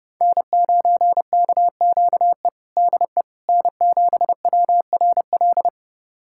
Morse code, heard as a keyed tone, about 32 wpm, 700 Hz. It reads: N9KQE DE N7WRL
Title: USA Callsign DE Callsign - 5 Characters - 30wpm